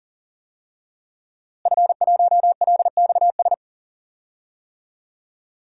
Morse code, heard as a keyed tone, about 40 wpm, 700 Hz. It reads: F1LXS
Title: DXCC-100 Call Signs - 5 Characters - Repeated 3x - 40wpm